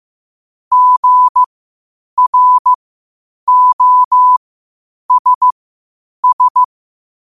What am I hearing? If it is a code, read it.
Morse code, heard as a keyed tone, about 15 wpm, spaced as for 8 wpm, 1000 Hz. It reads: GROSS
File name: qotdfiles / 8wpm